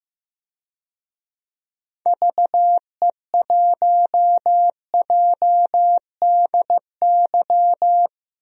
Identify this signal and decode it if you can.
Morse code, heard as a keyed tone, about 15 wpm, 700 Hz. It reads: VE1JDY
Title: DXCC-100 Call Signs - 6 Characters - Repeated 2x - 15wpm